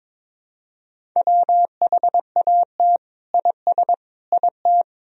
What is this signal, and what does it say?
Morse code, heard as a keyed tone, about 21 wpm, 700 Hz. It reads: WHAT IS IT
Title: Sentences from Top 100 Words and Abbreviations 22wpm